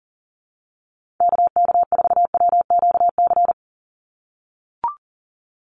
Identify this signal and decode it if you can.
Morse code, heard as a keyed tone, about 40 wpm, 700 Hz. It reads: KK4WQC
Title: USA Call Signs 40wpm